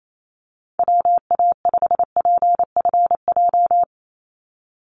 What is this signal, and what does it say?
Morse code, heard as a keyed tone, about 28 wpm, 700 Hz. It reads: WA5PFJ
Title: Call Signs - 28wpm